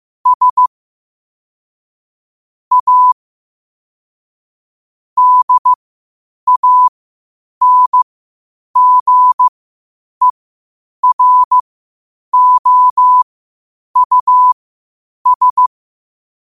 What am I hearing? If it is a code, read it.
Morse code, heard as a keyed tone, about 15 wpm, spaced as for 8 wpm, 1000 Hz. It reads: S A DANGEROUS
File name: 8wpm